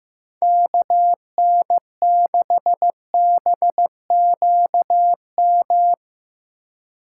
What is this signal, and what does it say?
Morse code, heard as a keyed tone, about 17 wpm, 700 Hz. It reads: KN6BQM
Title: Speed Racing - USA 2x3 Call Signs 15wpm